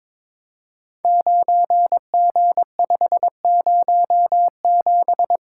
Morse code, heard as a keyed tone, about 22 wpm, 700 Hz. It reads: 9G507
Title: Code Groups - 5 Letter:Numbers - 22wpm